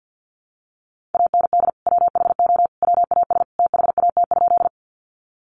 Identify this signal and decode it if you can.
Morse code, heard as a keyed tone, about 50 wpm, 700 Hz. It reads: AND WHO WAS THAT?
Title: Sentences from Top 100 Words 50wpm